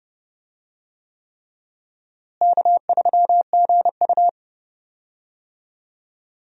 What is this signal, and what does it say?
Morse code, heard as a keyed tone, about 30 wpm, 700 Hz. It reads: K3GU
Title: USA Call Signs - 4 Characters - Repeated 3x - 30wpm